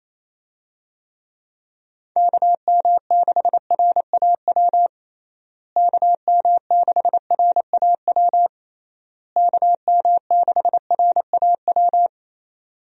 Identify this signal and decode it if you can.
Morse code, heard as a keyed tone, about 28 wpm, 700 Hz. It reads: KM6RAW KM6RAW KM6RAW
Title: USA Call Signs - 6 Characters - Repeated 3x - 28wpm